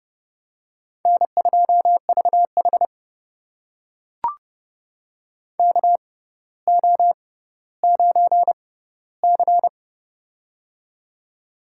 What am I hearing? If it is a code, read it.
Morse code, heard as a keyed tone, about 30 wpm, spaced as for 15 wpm, 700 Hz. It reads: N2VH KO9C
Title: USA Call Signs - 4 Characters - 6x Character Spacing - 30wpm